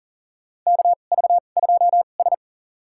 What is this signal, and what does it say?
Morse code, heard as a keyed tone, about 37 wpm, spaced as for 28 wpm, 700 Hz. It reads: KV2S
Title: USA Call Signs - 4 Characters - 2x Character Spacing - 40wpm